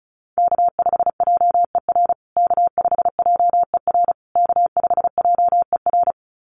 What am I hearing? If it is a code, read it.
Morse code, heard as a keyed tone, about 34 wpm, 700 Hz. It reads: K5JER K5JER K5JER
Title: USA Call Signs Repeated 3x 35wpm